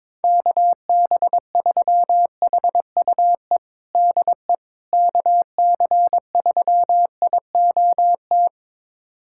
Morse code, heard as a keyed tone, about 22 wpm, 700 Hz. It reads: KB3HUE DE KC3IOT
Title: USA Callsign DE Callsign - 6 Characters - 22wpm